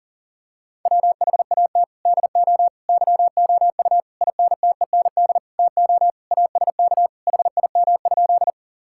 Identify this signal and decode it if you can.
Morse code, heard as a keyed tone, about 40 wpm, 700 Hz. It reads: WHAT DO YOU INTEND TO ASK HIM?
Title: Sentences from Top 1600 Words 40wpm